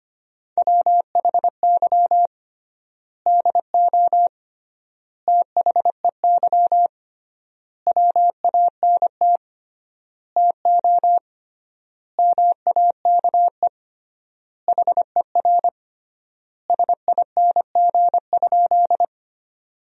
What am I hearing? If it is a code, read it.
Morse code, heard as a keyed tone, about 25 wpm, 700 Hz. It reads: WHY DO THEY WANT TO MAKE HER SING?
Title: 3x Word-Spacing - Sentences from Top 400 Words - 25wpm